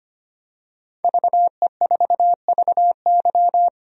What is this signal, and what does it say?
Morse code, heard as a keyed tone, about 25 wpm, 700 Hz. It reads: VE4VY
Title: DXCC-100 Call Signs - 5 Characters - 25wpm